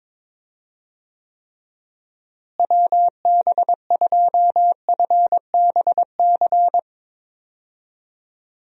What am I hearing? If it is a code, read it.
Morse code, heard as a keyed tone, about 22 wpm, 700 Hz. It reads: WB2FBC
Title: USA Call Signs - 6 Characters - Repeated 3x - 22wpm